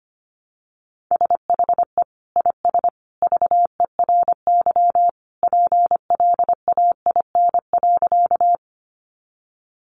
Morse code, heard as a keyed tone, about 25 wpm, 700 Hz. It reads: SHE IS VERY PLAIN.
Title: Sentences from Top 400 Words 25wpm